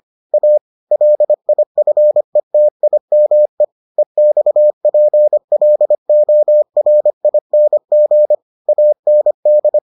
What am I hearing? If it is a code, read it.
Morse code, heard as a keyed tone, about 25 wpm, 600 Hz. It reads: A LIFETIME EXPLORING AND